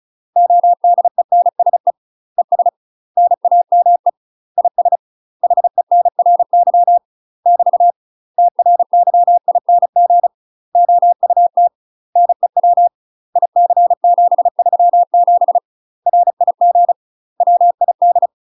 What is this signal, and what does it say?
Morse code, heard as a keyed tone, about 36 wpm, 700 Hz. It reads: ODENSE ES NAME IS HENRY <BT> TRYING OUT NEW IC737 RIG WID
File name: UK MASTER QSOs TRAINING / 35WPM / UK MASTER QSOs 0028